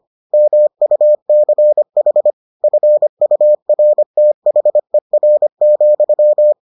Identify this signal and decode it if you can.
Morse code, heard as a keyed tone, about 25 wpm, 600 Hz. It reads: MUCH FURTHER,